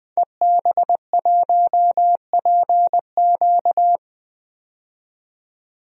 Morse code, heard as a keyed tone, about 20 wpm, 700 Hz. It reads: EB1PQ
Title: DXCC-100 Call Signs - 5 Characters - 20wpm